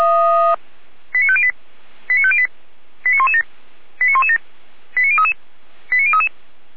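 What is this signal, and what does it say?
Five-tone selective call (ZVEI), heard as quick runs of five tones, tones of about 70 ms. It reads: 89508 89508 89117 89117 89033 89033
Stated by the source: Received by SDR as ZVEI five-tone selective call 89115.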